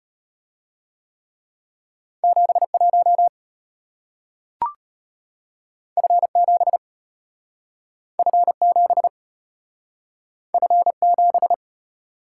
Morse code, heard as a keyed tone, about 34 wpm, 700 Hz. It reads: 71 F7 F7 F7